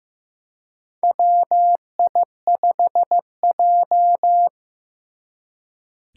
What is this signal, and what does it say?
Morse code, heard as a keyed tone, about 15 wpm, 700 Hz. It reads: WI5J
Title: Four Challenging Letter-Number Combinations 15wpm